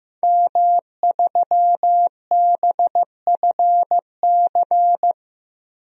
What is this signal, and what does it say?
Morse code, heard as a keyed tone, about 15 wpm, 700 Hz. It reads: M3BFC
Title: DXCC-100 Call Signs - 5 Characters - 15wpm